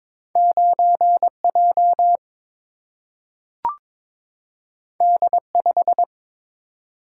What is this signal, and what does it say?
Morse code, heard as a keyed tone, about 22 wpm, 700 Hz. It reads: 9 J D 5
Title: Single-Letter Single-Number 22wpm